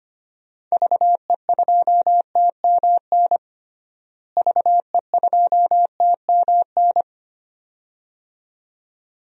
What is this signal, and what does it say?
Morse code, heard as a keyed tone, about 25 wpm, 700 Hz. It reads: VE2TMN VE2TMN
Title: DXCC-100 Call Signs - 6 Characters - Repeated 2x - 25wpm